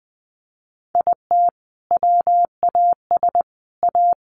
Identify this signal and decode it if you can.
Morse code, heard as a keyed tone, about 20 wpm, 700 Hz. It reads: IT WAS A
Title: Sentences from Top 1500 Words 20wpm